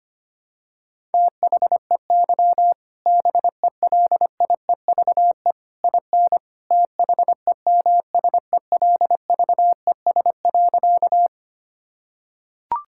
Morse code, heard as a keyed tone, about 25 wpm, 700 Hz. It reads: THEY BELIEVE IN THEMSELVES.
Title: Sentences from Top 1000 Words - Review Part 5 - 25wpm